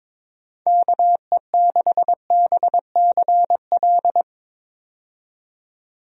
Morse code, heard as a keyed tone, about 22 wpm, 700 Hz. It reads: KE6BCL
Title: USA Call Signs - 6 Characters - 22wpm